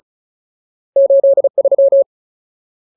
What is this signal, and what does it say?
Morse code, heard as a keyed tone, about 34 wpm, 550 Hz. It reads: 83
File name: Ultimate Word List / 35WPM Ultimate